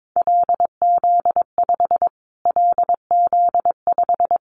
Morse code, heard as a keyed tone, about 22 wpm, 700 Hz. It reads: LZ5 LZ5
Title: Callsign Prefixes for Top 25 DXCC - Repeated 3x - 22wpm